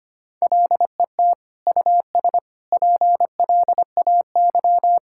Morse code, heard as a keyed tone, about 25 wpm, 700 Hz. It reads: LET US PLAY
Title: Sentences from Top 1600 Words 25wpm